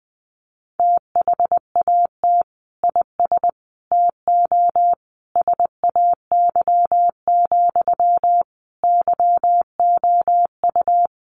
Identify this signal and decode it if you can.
Morse code, heard as a keyed tone, about 20 wpm, 700 Hz. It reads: THAT IS TO SAY, YOU
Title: Sentences from Top 700 Words 20wpm